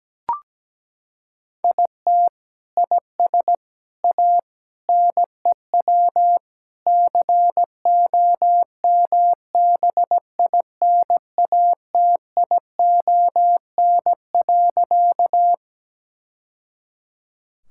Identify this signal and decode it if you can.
Morse code, heard as a keyed tone, about 17 wpm, 700 Hz. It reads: IT IS A NEW COMBINATION.
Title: Sentences from Top 1000 Words - Review Part 6 - 17wpm